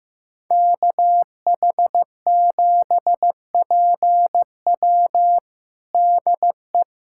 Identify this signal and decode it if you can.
Morse code, heard as a keyed tone, about 15 wpm, 700 Hz. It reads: KH7PW DE
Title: USA Callsign DE Callsign - 5 Characters - 15wpm